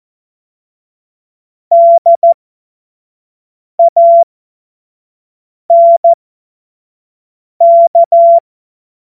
Morse code, heard as a keyed tone, about 14 wpm, spaced as for 5 wpm, 680 Hz. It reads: DANK
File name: Morsekurs / Witze / Witz 14. Dezember 2021, 25 F70